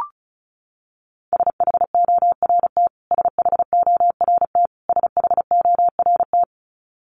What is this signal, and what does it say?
Morse code, heard as a keyed tone, about 34 wpm, 700 Hz. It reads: SHORT SHORT SHORT
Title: Top 500 Words Repeated 35wpm